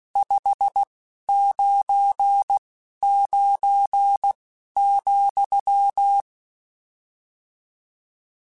Morse code, heard as a keyed tone, about 16 wpm, spaced as for 11 wpm, 800 Hz. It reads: 599,